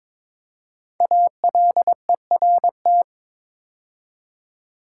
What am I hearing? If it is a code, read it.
Morse code, heard as a keyed tone, about 22 wpm, 690 Hz. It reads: ALERT
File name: 5 Letter Words - First 3 Letters Given - 22wpm